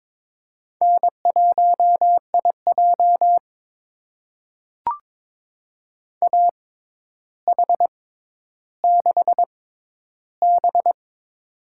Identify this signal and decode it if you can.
Morse code, heard as a keyed tone, about 22 wpm, spaced as for 11 wpm, 700 Hz. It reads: N1IJ AH6B
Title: USA Call Signs - 4 Characters - 6x Character Spacing - 22wpm